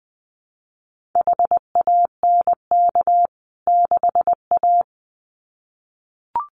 Morse code, heard as a keyed tone, about 20 wpm, 700 Hz. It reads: HANK 6A
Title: CWT Exchange 20wpm